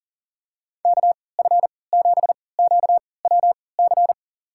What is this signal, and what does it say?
Morse code, heard as a keyed tone, about 40 wpm, spaced as for 23 wpm, 700 Hz. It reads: KF7QWC
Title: USA Call Signs - 3x Character Spacing - 40wpm